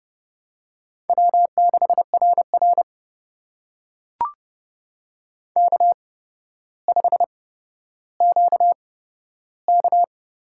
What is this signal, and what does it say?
Morse code, heard as a keyed tone, about 30 wpm, spaced as for 13 wpm, 700 Hz. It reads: W6RR K5QK
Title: USA Call Signs - 4 Characters - 8x Character Spacing - 30wpm